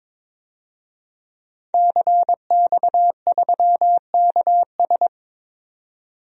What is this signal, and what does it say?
Morse code, heard as a keyed tone, about 22 wpm, 700 Hz. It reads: CX3KS
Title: DXCC-100 Call Signs - 5 Characters - 22wpm